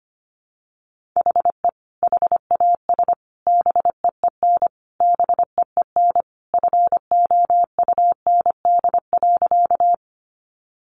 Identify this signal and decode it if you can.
Morse code, heard as a keyed tone, about 25 wpm, 700 Hz. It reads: HE HAS BEEN BEEN FOUND.
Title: Sentences from Top 300 Words 25wpm